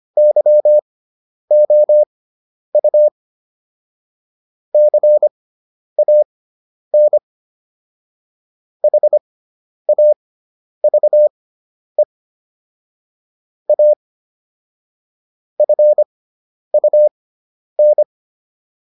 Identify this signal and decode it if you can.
Morse code, heard as a keyed tone, about 25 wpm, spaced as for 10 wpm, 600 Hz. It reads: YOU CAN HAVE A FUN